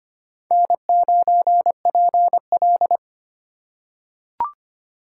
Morse code, heard as a keyed tone, about 25 wpm, 700 Hz. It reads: N9PL
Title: USA Call Signs - 4 Characters - 3x Character Spacing - 25wpm